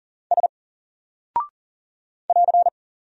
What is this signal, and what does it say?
Morse code, heard as a keyed tone, about 40 wpm, 700 Hz. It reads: S <AR>
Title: Single Character (Letter:Number:Punctuation:Prosign) 40wpm